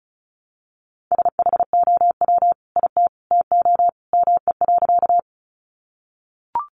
Morse code, heard as a keyed tone, about 34 wpm, 700 Hz. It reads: SHOW IT TO ME.
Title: Sentences from Top 200 Words 35wpm